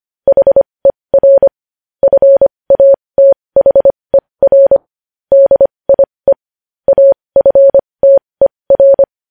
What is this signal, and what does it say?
Morse code, heard as a keyed tone, about 25 wpm, spaced as for 20 wpm, 550 Hz. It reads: HER FATHER DIE AFTER
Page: news-headlines-in-morse-code-at-20-wpm